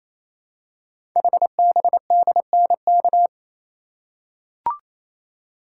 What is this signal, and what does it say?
Morse code, heard as a keyed tone, about 27 wpm, 700 Hz. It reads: HBDNK
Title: Code Groups - 5 Letters - 28wpm